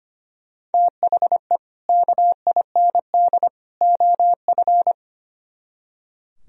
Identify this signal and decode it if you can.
Morse code, heard as a keyed tone, about 33 wpm, 700 Hz. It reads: THE KIND OF